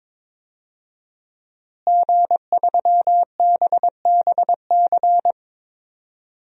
Morse code, heard as a keyed tone, about 22 wpm, 700 Hz. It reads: G3BBC
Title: DXCC-100 Call Signs - 5 Characters - Repeated 3x - 22wpm